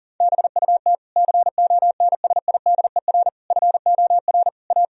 Morse code, heard as a keyed tone, about 40 wpm, 700 Hz. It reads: BUT CONSIDER FOR A
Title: Sentences from Top 1000 Words - Review Part 4 - 40wpm